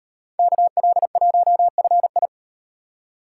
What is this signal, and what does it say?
Morse code, heard as a keyed tone, about 38 wpm, 700 Hz. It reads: KL1FI